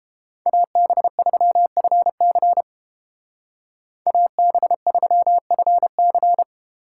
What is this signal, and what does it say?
Morse code, heard as a keyed tone, about 30 wpm, 700 Hz. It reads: AB3FC AB3FC